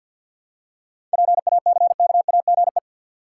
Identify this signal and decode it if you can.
Morse code, heard as a keyed tone, about 52 wpm, 700 Hz. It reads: PACKAGE
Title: Seven Letter Words - 50wpm